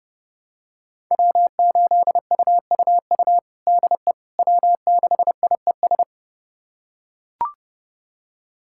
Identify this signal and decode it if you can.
Morse code, heard as a keyed tone, about 32 wpm, 700 Hz. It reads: W8UUU DE W6IES
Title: USA Callsign DE Callsign - 5 Characters - 30wpm